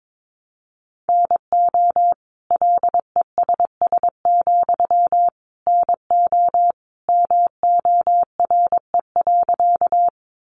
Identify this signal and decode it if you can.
Morse code, heard as a keyed tone, about 22 wpm, 700 Hz. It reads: NO LESS, NO MORE.
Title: Sentences from Top 400 Words 22wpm